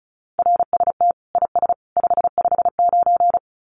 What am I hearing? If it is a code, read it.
Morse code, heard as a keyed tone, about 34 wpm, 700 Hz. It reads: RST IS 559
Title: Common QSO Phrases 35wpm